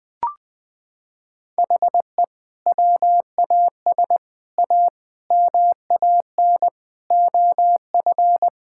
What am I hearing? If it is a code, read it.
Morse code, heard as a keyed tone, about 21 wpm, 700 Hz. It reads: HE WAS A MAN OF
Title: Sentences from Top 1000 Words - Review Part 6 - 20wpm